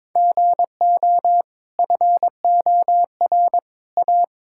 Morse code, heard as a keyed tone, about 21 wpm, 690 Hz. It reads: GO FOR A
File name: practice / 22wpm / Easy Sentences with Hints - 22wpm